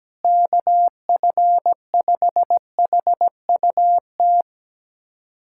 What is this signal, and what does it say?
Morse code, heard as a keyed tone, about 17 wpm, 700 Hz. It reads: KF5HUT
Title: USA Callsign DE Callsign - 6 Characters - 17wpm